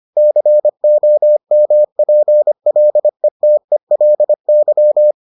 Morse code, heard as a keyed tone, about 25 wpm, 600 Hz. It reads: COMPLETELY